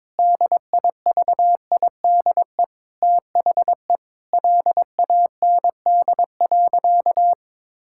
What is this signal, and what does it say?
Morse code, heard as a keyed tone, about 22 wpm, 700 Hz. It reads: DIVIDE THE LAND.
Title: Sentences from Top 1600 Words 22wpm